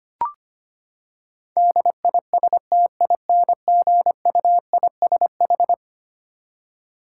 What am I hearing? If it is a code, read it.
Morse code, heard as a keyed tone, about 25 wpm, 700 Hz. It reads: DISTINGUISH
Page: morse-code-ninja-practice-25wpm